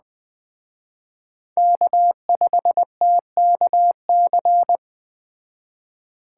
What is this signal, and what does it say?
Morse code, heard as a keyed tone, about 20 wpm, 700 Hz. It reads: K5TKC
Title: Speed Racing - USA 1x3 Call Signs 17wpm